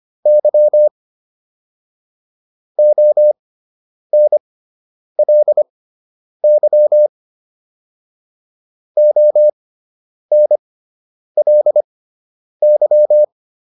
Morse code, heard as a keyed tone, about 25 wpm, spaced as for 9 wpm, 600 Hz. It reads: Y ONLY ONLY